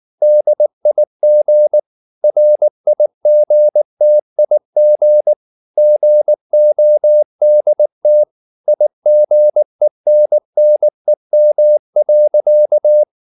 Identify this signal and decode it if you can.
Morse code, heard as a keyed tone, about 19 wpm, 600 Hz. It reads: DIG RIGTIG GODT IGENNEM.
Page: Ferie 19 wpm